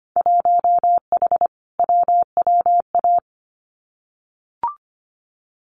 Morse code, heard as a keyed tone, about 25 wpm, 700 Hz. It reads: 1H WWA
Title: Winter Field Day Exchanges 25wpm